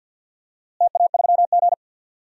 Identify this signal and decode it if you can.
Morse code, heard as a keyed tone, about 53 wpm, 700 Hz. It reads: TA3G
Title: DXCC-100 Call Signs - 4 Characters - Repeated 3x - 50wpm